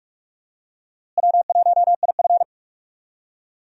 Morse code, heard as a keyed tone, about 47 wpm, 700 Hz. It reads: W1IF
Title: USA Call Signs - 4 Characters - Repeated 3x - 45wpm